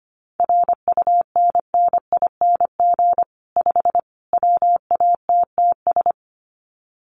Morse code, heard as a keyed tone, about 25 wpm, 700 Hz. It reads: RUNNING 5 WATTS
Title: Common QSO Phrases 25wpm